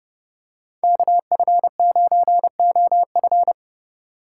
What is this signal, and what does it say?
Morse code, heard as a keyed tone, about 27 wpm, 700 Hz. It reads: KF9OF